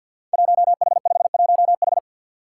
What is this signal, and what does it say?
Morse code, heard as a keyed tone, about 47 wpm, 700 Hz. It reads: 1H51H
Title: Five Challenging Letter-Number Combinations 50wpm